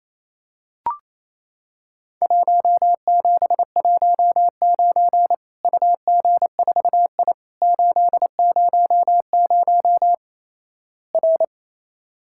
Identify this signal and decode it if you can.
Morse code, heard as a keyed tone, about 28 wpm, 700 Hz. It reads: 1719 UG4I 800 R
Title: Worked All Europe DX Contest - QTC Exchange - 28wpm